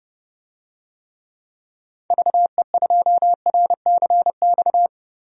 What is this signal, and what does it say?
Morse code, heard as a keyed tone, about 32 wpm, 700 Hz. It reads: VE2RCX
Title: DXCC-100 Call Signs - 6 Characters - Repeated 3x - 30wpm